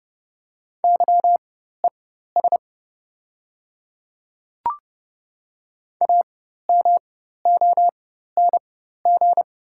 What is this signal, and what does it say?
Morse code, heard as a keyed tone, about 30 wpm, spaced as for 14 wpm, 690 Hz. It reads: YES AMONG
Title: 4x Character Spacing - Top 500 Words - 30wpm